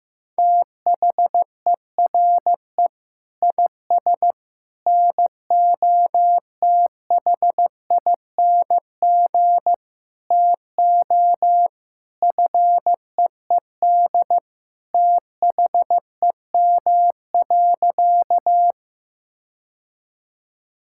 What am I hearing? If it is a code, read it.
Morse code, heard as a keyed tone, about 15 wpm, 700 Hz. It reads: THERE IS NOTHING TO FEED THEM.
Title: Sentences from Top 1600 Words 15wpm